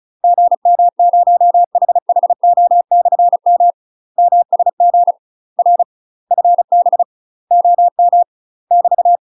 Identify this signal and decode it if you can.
Morse code, heard as a keyed tone, about 36 wpm, 700 Hz. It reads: GM0HHO/M MSG R FB OM <BT>
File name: UK MASTER QSOs TRAINING / 35WPM / UK MASTER QSOs 0014